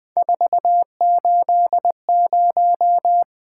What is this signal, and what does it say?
Morse code, heard as a keyed tone, about 20 wpm, 700 Hz. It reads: 480
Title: Three Letter-Number Combinations - 20wpm